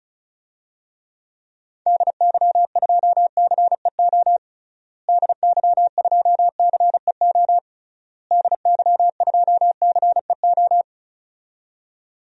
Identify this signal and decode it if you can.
Morse code, heard as a keyed tone, about 36 wpm, 700 Hz. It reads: DY2CEO DY2CEO DY2CEO
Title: DXCC-100 Call Signs - 6 Characters - Repeated 3x - 35wpm